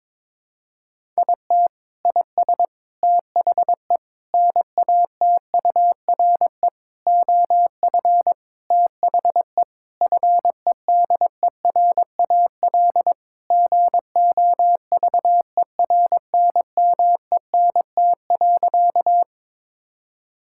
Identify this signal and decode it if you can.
Morse code, heard as a keyed tone, about 22 wpm, 700 Hz. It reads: IT IS THE NATURE OF THE FEDERAL GOVERNMENT.
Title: Sentences from Top 1000 Words - Review Part 5 - 22wpm